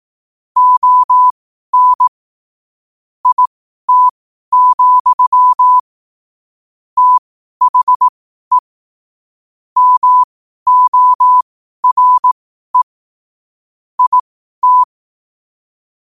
Morse code, heard as a keyed tone, about 18 wpm, spaced as for 12 wpm, 1000 Hz. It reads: ON IT, THE MORE IT
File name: qotdfiles / 12wpm